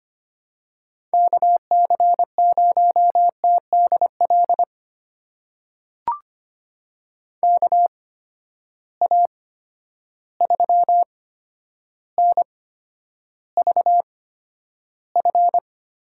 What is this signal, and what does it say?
Morse code, heard as a keyed tone, about 25 wpm, spaced as for 11 wpm, 700 Hz. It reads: KC0TDL KA3NVF
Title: USA Call Signs - 6 Characters - 8x Character Spacing - 25wpm